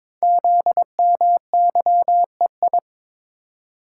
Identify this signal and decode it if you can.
Morse code, heard as a keyed tone, about 22 wpm, 700 Hz. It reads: ZMYEI
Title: Code Groups - 5 Letters - 22wpm